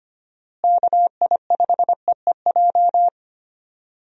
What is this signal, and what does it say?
Morse code, heard as a keyed tone, about 25 wpm, 700 Hz. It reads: KI5EEJ
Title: USA Call Signs - 6 Characters - Repeated 3x - 25wpm